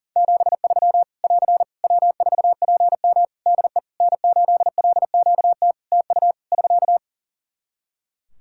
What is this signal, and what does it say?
Morse code, heard as a keyed tone, about 40 wpm, 700 Hz. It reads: 73 <AR> W4PM DE N8LQT TU <SK>
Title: Practice QSO - Exchange 4 - Ending - 40wpm